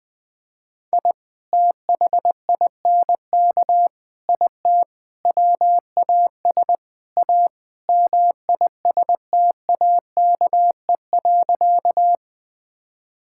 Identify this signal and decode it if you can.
Morse code, heard as a keyed tone, about 20 wpm, 700 Hz. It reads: I THINK IT WAS A MISTAKE.
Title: Sentences with 5-Word Phrases - 20wpm